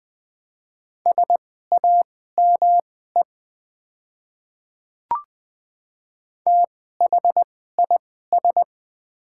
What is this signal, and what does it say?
Morse code, heard as a keyed tone, about 20 wpm, spaced as for 14 wpm, 700 Hz. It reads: SAME THIS
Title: 2x Character Spacing - Top 300 Words - 20wpm